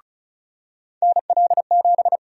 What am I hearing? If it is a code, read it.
Morse code, heard as a keyed tone, about 34 wpm, 700 Hz. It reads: NL7
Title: USA Call Sign Prefixes - 35wpm